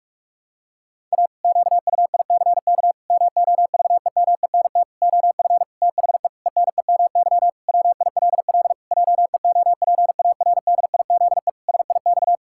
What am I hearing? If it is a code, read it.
Morse code, heard as a keyed tone, about 41 wpm, 700 Hz. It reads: A QUICK MOVEMENT OF THE ENEMY WILL JEOPARDIZE SIX
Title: Pangram Challenge v2 - 45wpm